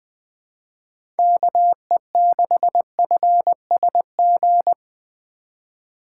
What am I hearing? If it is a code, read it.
Morse code, heard as a keyed tone, about 20 wpm, 700 Hz. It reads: KE6FSG
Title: USA Call Signs - 6 Characters - 3x Character Spacing - 20wpm